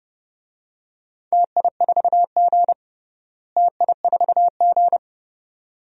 Morse code, heard as a keyed tone, about 30 wpm, 700 Hz. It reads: TI4G TI4G
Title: DXCC-100 Call Signs - 4 Characters - Repeated 2x - 30wpm